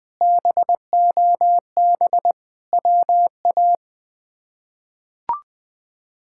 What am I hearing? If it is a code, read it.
Morse code, heard as a keyed tone, about 21 wpm, 700 Hz. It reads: BOB WA
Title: K1USN SST QSO Practice - 20wpm